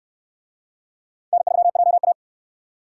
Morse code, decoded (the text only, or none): N3JA